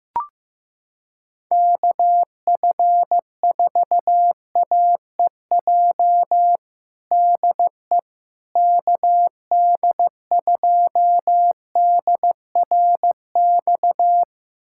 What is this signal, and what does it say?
Morse code, heard as a keyed tone, about 15 wpm, 700 Hz. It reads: KF4AEJ DE KD2DRX
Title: USA Callsign DE Callsign - 6 Characters - 15wpm